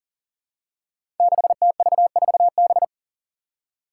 Morse code, heard as a keyed tone, about 40 wpm, 700 Hz. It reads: 6TV4B
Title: Five Challenging Letter-Number Combinations 40wpm